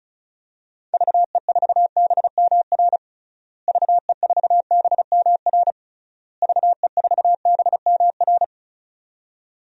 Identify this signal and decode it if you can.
Morse code, heard as a keyed tone, about 35 wpm, 700 Hz. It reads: VE4BMR VE4BMR VE4BMR
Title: DXCC-100 Call Signs - 6 Characters - Repeated 3x - 35wpm